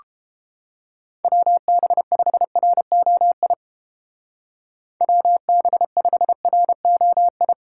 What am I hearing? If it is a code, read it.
Morse code, heard as a keyed tone, about 33 wpm, 700 Hz. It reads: WB5ROI WB5ROI